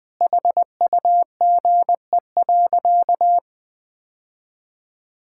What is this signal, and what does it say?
Morse code, heard as a keyed tone, about 20 wpm, 700 Hz. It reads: HUGE.
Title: Sentences with 6-Word Phrases - 20wpm